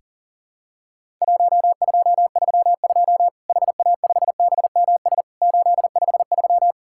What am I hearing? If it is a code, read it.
Morse code, heard as a keyed tone, about 40 wpm, 700 Hz. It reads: 1232 HA5BMS 853
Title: Worked All Europe DX Contest - QTC Exchange - 40wpm